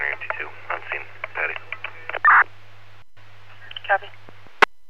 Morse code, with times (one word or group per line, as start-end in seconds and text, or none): none